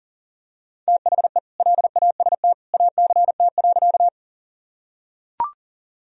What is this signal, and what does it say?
Morse code, heard as a keyed tone, about 40 wpm, 700 Hz. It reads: THE LAST ACT.
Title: Sentences from Top 1000 Words - Review Part 1 - 40wpm